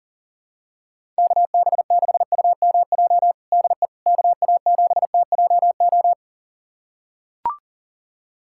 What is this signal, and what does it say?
Morse code, heard as a keyed tone, about 40 wpm, 700 Hz. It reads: KB6UMJ DE KA7TJO
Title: USA Callsign DE Callsign - 6 Characters - 40wpm